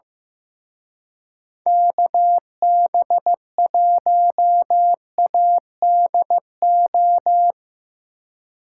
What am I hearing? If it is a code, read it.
Morse code, heard as a keyed tone, about 17 wpm, 700 Hz. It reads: KB1ADO
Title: Speed Racing - USA 2x3 Call Signs 15wpm